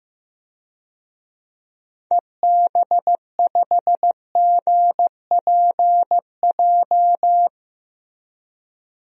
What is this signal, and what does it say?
Morse code, heard as a keyed tone, about 15 wpm, 700 Hz. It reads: EB5GPJ
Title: DXCC-100 Call Signs - 6 Characters - Repeated 2x - 15wpm